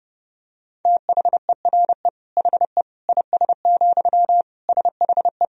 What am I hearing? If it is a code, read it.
Morse code, heard as a keyed tone, about 30 wpm, 700 Hz. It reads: THERE HE IS, SHE
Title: Sentences from Top 100 Words and Abbreviations 30wpm